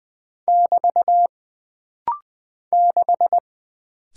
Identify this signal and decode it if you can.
Morse code, heard as a keyed tone, about 20 wpm, 700 Hz. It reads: <BT> 6
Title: Single Character (Letter:Number:Punctuation:Prosign) - Rapid-Fire - 20wpm